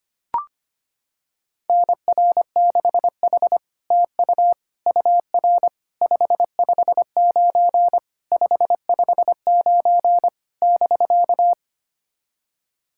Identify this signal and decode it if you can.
Morse code, heard as a keyed tone, about 25 wpm, 700 Hz. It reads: NR6H TU UR 559 559 <BK>
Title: SOTA QSO Practice 25wpm